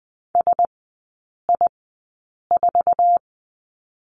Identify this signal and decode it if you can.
Morse code, heard as a keyed tone, about 20 wpm, spaced as for 8 wpm, 700 Hz. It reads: SI4
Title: 100 Lesson 17e - Letter u - Three Characters 20wpm